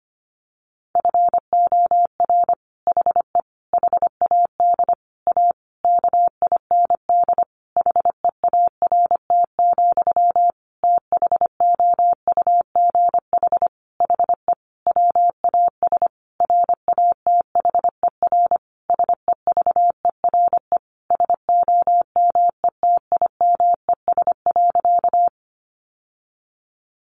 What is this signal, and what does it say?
Morse code, heard as a keyed tone, about 25 wpm, 700 Hz. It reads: FOR HE HAD A KIND HEART, THOUGH HE WAS RATHER SEVERE SOMETIMES.
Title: Sentences from Top 1400 Words 25wpm